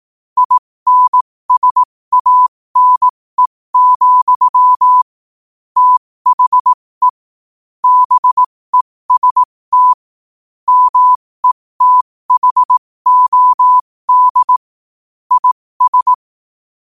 Morse code, heard as a keyed tone, about 18 wpm, spaced as for 15 wpm, 1000 Hz. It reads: INSANE, THE BEST METHOD IS
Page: New quotes every day in morse code at 15 Words per minute.